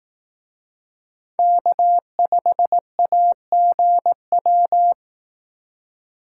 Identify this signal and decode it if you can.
Morse code, heard as a keyed tone, about 18 wpm, 700 Hz. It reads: K5AGW
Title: Speed Racing - USA Call Signs 15wpm